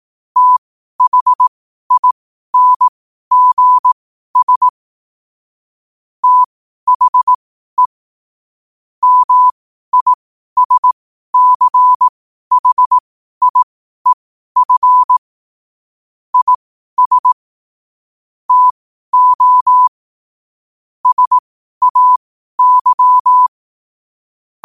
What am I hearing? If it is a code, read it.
Morse code, heard as a keyed tone, about 18 wpm, spaced as for 12 wpm, 1000 Hz. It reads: THINGS THE MISCHIEF IS TO SAY
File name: qotdfiles / 12wpm